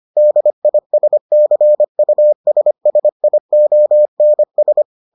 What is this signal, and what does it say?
Morse code, heard as a keyed tone, about 25 wpm, 600 Hz. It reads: DISCUSSIONS